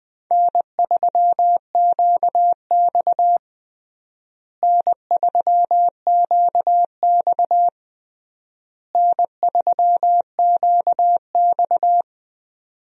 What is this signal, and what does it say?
Morse code, heard as a keyed tone, about 21 wpm, 700 Hz. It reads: N3QX N3QX N3QX
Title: USA Call Signs - 4 Characters - Repeated 3x - 20wpm